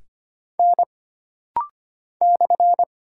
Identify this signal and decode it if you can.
Morse code, heard as a keyed tone, about 25 wpm, 700 Hz. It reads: N /
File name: Single Character (Letter:Number:Punctuation:Prosign) - Rapid-Fire - 25wpm